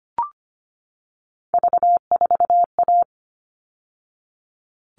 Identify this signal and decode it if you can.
Morse code, heard as a keyed tone, about 25 wpm, 700 Hz. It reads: V4A
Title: All Cut Numbers - 3-Digit 25wpm